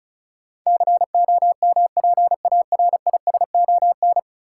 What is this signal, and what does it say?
Morse code, heard as a keyed tone, about 36 wpm, 700 Hz. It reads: COMPARISON
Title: Sentences from Top 1600 Words - New Word Drill - 35wpm